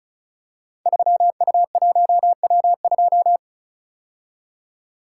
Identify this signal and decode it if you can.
Morse code, heard as a keyed tone, about 36 wpm, 690 Hz. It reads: 3U1W2
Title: Code Groups - 5 Letter:Numbers - 35wpm